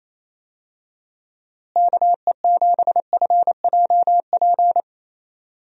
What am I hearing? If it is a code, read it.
Morse code, heard as a keyed tone, about 27 wpm, 700 Hz. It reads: KE7FJP
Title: USA Call Signs - 6 Characters - Repeated 3x - 28wpm